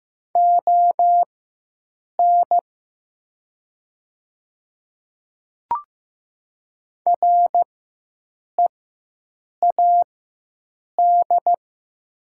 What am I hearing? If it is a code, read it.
Morse code, heard as a keyed tone, about 15 wpm, spaced as for 7 wpm, 700 Hz. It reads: ON READ
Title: 4x Character Spacing - Top 200 Words - 15wpm